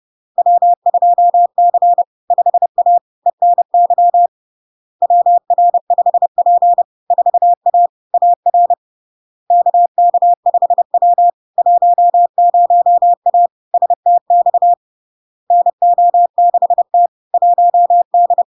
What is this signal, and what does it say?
Morse code, heard as a keyed tone, about 30 wpm, 700 Hz. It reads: W2C 5A ENY WR5P 4A AR KK5W 10A STX NO6T 1D
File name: LARK-FD-2022-Log-in-CW-30wpm-700-Hz